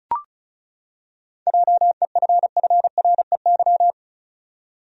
Morse code, heard as a keyed tone, about 36 wpm, 700 Hz. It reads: JEFFREY
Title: USA Common Names - 35wpm